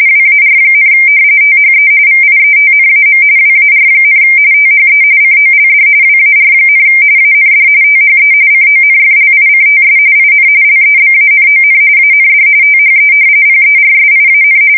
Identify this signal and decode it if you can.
RTTY, about 45.45 baud. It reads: WAR, THE VIETNAM WAR, THE COLD WAR, AND THE CONFLICTS IN LEBANON AND THE PERSIAN
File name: fsk170 / battleshipnj